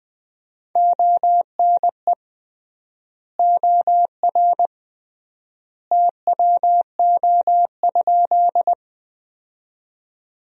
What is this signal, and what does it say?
Morse code, heard as a keyed tone, about 20 wpm, 700 Hz. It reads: ONE OR TWO?
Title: 3x Word-Spacing - Sentences from Top 100 Words - 20wpm